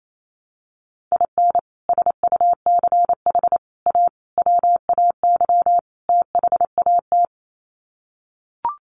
Morse code, heard as a keyed tone, about 28 wpm, 700 Hz. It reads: IN SUCH A WAY THAT
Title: Sets of 5 Words 28wpm